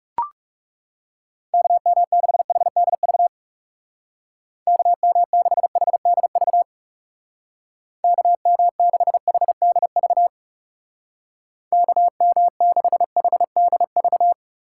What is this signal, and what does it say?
Morse code, heard as a keyed tone, about 37 wpm, 700 Hz. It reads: KM6HDV KM6HDV KM6HDV KM6HDV
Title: Speed Racing - USA 2x3 Call Signs 30wpm